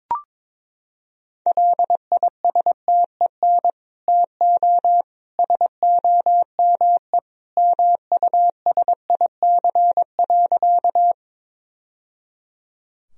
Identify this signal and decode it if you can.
Morse code, heard as a keyed tone, about 22 wpm, 700 Hz. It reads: LISTEN TO SOME MUSIC.
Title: Sentences from Top 1000 Words - Review Part 1 - 22wpm